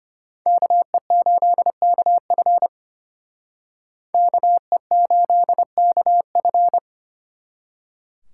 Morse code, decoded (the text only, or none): KE8KF KE8KF